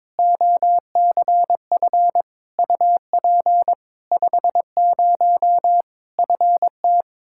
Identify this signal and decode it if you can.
Morse code, heard as a keyed tone, about 22 wpm, 700 Hz. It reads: OCF UP 50 FT
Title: Practice QSO - Exchange 2 - Info - 22wpm